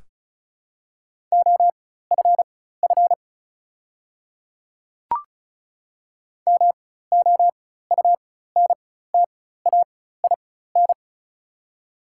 Morse code, heard as a keyed tone, about 36 wpm, spaced as for 16 wpm, 700 Hz. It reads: OFF MOUNTAIN
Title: 4x Character Spacing - Top 300 Words - 35wpm